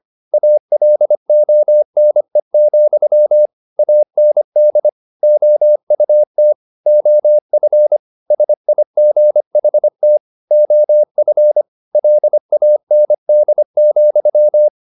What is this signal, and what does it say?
Morse code, heard as a keyed tone, about 25 wpm, 600 Hz. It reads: ALONE, AND OUT OF SIGHT OF LAND,
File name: the old man and the sea part 1 25wpm